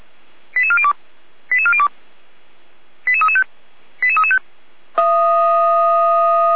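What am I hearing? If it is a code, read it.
Five-tone selective call (ZVEI), heard as quick runs of five tones, tones of about 70 ms. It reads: 88402 88402 88335 88335
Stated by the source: Five-tone ZVEI selective call received with an SDR, Feuerwehr Orlamünde.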